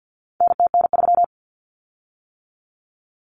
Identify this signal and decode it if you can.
Morse code, heard as a keyed tone, about 50 wpm, 700 Hz. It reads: NTN3
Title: Common Cut Numbers - 4-Digit 50wpm